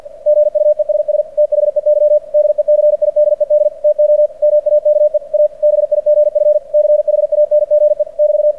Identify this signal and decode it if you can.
Morse code, heard as a keyed tone, about 48 wpm, 600 Hz. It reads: GUESS THEY DECIDED TO MAKE A BIGF CHANGE <BT>
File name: archived / CRSnet